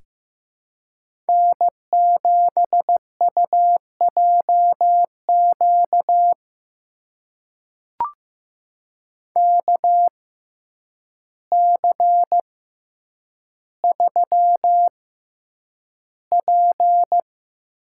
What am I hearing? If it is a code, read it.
Morse code, heard as a keyed tone, about 15 wpm, spaced as for 8 wpm, 700 Hz. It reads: N7UJQ KC3P
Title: USA Call Signs - 5 Characters - 6x Character Spacing - 15wpm